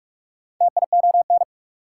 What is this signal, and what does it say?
Morse code, heard as a keyed tone, about 60 wpm, spaced as for 53 wpm, 690 Hz. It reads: TION
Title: Speed-Racing Prefixes and Suffixes 45wpm